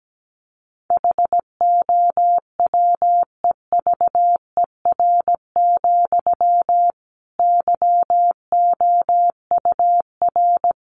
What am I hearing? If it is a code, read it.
Morse code, heard as a keyed tone, about 17 wpm, 700 Hz. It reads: HOWEVER, YOUR
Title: Sentences from Top 1200 Words 17wpm